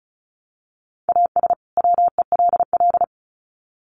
Morse code, heard as a keyed tone, about 34 wpm, 700 Hz. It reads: AS WELL
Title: Sets of 2 Words 35wpm